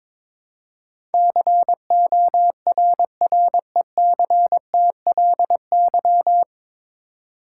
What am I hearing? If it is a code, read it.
Morse code, heard as a keyed tone, about 22 wpm, 700 Hz. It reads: CORRECTLY
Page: Morse Code Ninja practice 22wpm